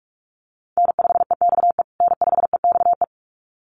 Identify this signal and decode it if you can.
Morse code, heard as a keyed tone, about 44 wpm, 690 Hz. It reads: N5EXE N5EXE
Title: USA Call Signs Repeated 2x 45wpm